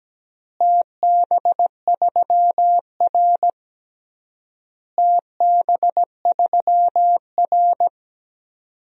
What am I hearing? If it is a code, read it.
Morse code, heard as a keyed tone, about 17 wpm, 700 Hz. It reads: TB3R TB3R
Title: DXCC-100 Call Signs - 4 Characters - Repeated 3x - 17wpm